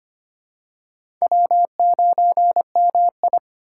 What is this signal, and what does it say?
Morse code, heard as a keyed tone, about 25 wpm, 700 Hz. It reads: W9MI
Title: USA Call Signs - 4 Characters - 3x Character Spacing - 25wpm